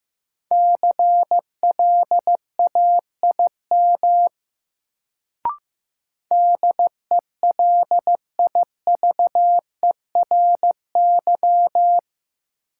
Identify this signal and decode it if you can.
Morse code, heard as a keyed tone, about 15 wpm, 700 Hz. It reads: CLAIM DELIVERY
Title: Sentences from Top 1500 Words - New Word Drill Alt - 15wpm